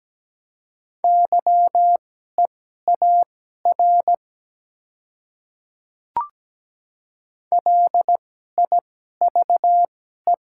Morse code, heard as a keyed tone, about 17 wpm, spaced as for 12 wpm, 700 Hz. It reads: YEAR LIVE
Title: 2x Character Spacing - Top 200 Words - 17wpm